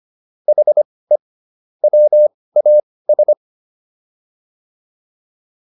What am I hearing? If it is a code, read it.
Morse code, heard as a keyed tone, about 25 wpm, spaced as for 18 wpm, 600 Hz. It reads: HE WAS
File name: WD104 18